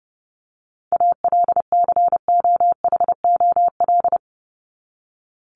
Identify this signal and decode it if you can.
Morse code, heard as a keyed tone, about 30 wpm, 700 Hz. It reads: ALCOHOL
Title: Top 1251-1500 Words 30wpm